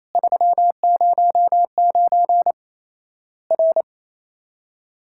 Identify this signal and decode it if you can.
Morse code, heard as a keyed tone, about 27 wpm, 700 Hz. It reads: 309 R
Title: Worked All Europe DX Contest - QTC Exchange - 28wpm